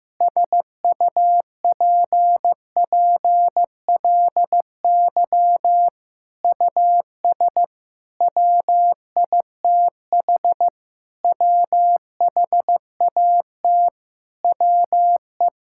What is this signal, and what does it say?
Morse code, heard as a keyed tone, about 15 wpm, 700 Hz. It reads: SUPPLY US WITH WHAT WE
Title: Sentences from Top 1000 Words - Review Part 5 - 15wpm